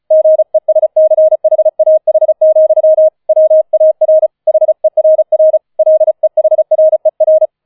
Morse code, heard as a keyed tone, about 34 wpm, 630 Hz. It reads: GESCHAH, WAR HERR LEHRER
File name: 004 Vierter Streich 35wpm